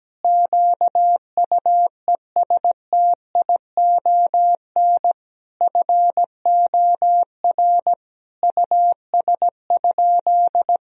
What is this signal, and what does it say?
Morse code, heard as a keyed tone, about 17 wpm, 700 Hz. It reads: QUESTION FOR US?
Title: Sentences from Top 1000 Words - Review Part 3 - 17wpm